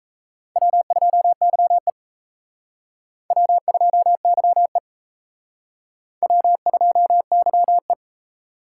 Morse code, heard as a keyed tone, about 35 wpm, 700 Hz. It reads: W2YE W2YE W2YE
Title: Speed Racing - USA 1x2 Call Signs 28wpm